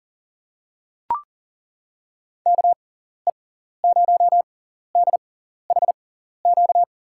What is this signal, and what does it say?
Morse code, heard as a keyed tone, about 40 wpm, spaced as for 14 wpm, 700 Hz. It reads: KE0DHQ
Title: USA Call Signs - 6x Character Spacing - 40wpm